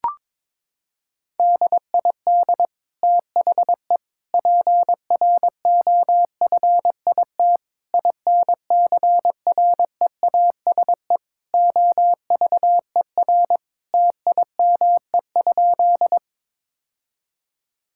Morse code, heard as a keyed tone, about 22 wpm, 700 Hz. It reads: DID THE PROFIT INCREASE OVER TIME?
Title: Sentences from Top 1000 Words - Review Part 5 - 22wpm